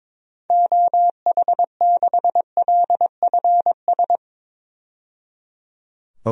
Morse code, heard as a keyed tone, about 22 wpm, 700 Hz. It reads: OH6LFS
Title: DXCC-100 Call Signs - 6 Characters - 22wpm